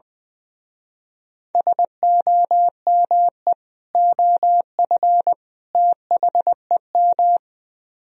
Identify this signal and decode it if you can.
Morse code, heard as a keyed tone, about 20 wpm, 700 Hz. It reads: SOME OF THEM
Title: Speed-Racing - Sets of 3 Words 20wpm